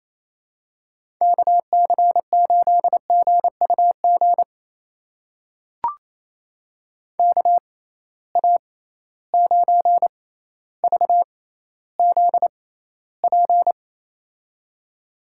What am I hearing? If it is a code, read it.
Morse code, heard as a keyed tone, about 28 wpm, spaced as for 14 wpm, 700 Hz. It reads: KC8GUG KA9VZP
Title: USA Call Signs - 6 Characters - 6x Character Spacing - 28wpm